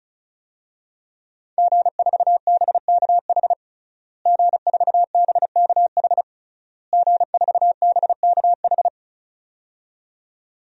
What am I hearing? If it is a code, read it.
Morse code, heard as a keyed tone, about 36 wpm, 700 Hz. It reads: G4BKH G4BKH G4BKH
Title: DXCC-100 Call Signs - 5 Characters - Repeated 3x - 35wpm